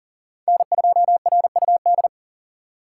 Morse code, heard as a keyed tone, about 40 wpm, 700 Hz. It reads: N2RUD
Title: USA Call Signs - 5 Characters - 2x Character Spacing - 40wpm